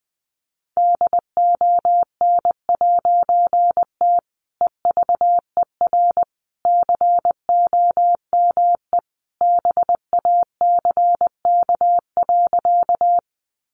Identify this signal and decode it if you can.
Morse code, heard as a keyed tone, about 20 wpm, 700 Hz. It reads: DON'T EVER COME BACK.
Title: A Sound Of Thunder 20wpm